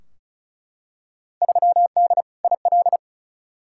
Morse code, heard as a keyed tone, about 36 wpm, 690 Hz. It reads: 3D IL
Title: Field Day Exchanges Encore - 35wpm